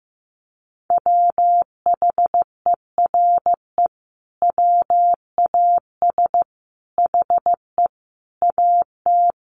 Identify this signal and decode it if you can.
Morse code, heard as a keyed tone, about 15 wpm, 700 Hz. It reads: WHERE WAS HE AT
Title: Sentences from Top 1000 Words 15wpm